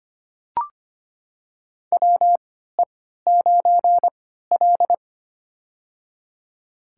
Morse code, heard as a keyed tone, about 25 wpm, spaced as for 14 wpm, 700 Hz. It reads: WE9L
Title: USA Call Signs - 3x Character Spacing - 25wpm